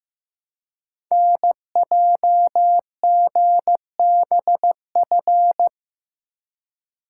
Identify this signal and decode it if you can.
Morse code, heard as a keyed tone, about 15 wpm, 700 Hz. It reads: NJGBF
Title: Code Groups - 5 Letters - 15wpm